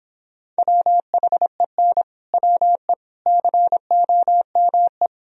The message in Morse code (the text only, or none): WHEN WE COME